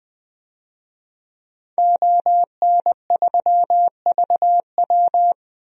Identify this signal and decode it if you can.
Morse code, heard as a keyed tone, about 20 wpm, 700 Hz. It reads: ON3VW
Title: DXCC-100 Call Signs - 5 Characters - 20wpm